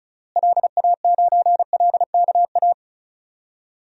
Morse code, heard as a keyed tone, about 36 wpm, 700 Hz. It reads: LA9LKA
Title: DXCC-100 Call Signs - 6 Characters - Repeated 2x - 35wpm